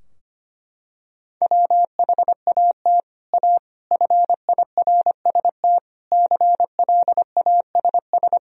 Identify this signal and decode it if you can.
Morse code, heard as a keyed tone, about 25 wpm, 700 Hz. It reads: WHAT A FIRST CLASS
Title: Sentences from Top 1000 Words - Review Part 2 - 25wpm